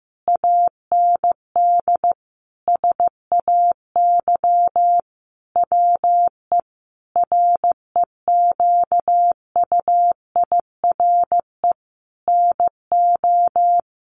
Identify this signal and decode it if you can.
Morse code, heard as a keyed tone, about 15 wpm, 700 Hz. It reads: AND SAY WE REQUIRE NO
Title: Sentences from Top 1500 Words 15wpm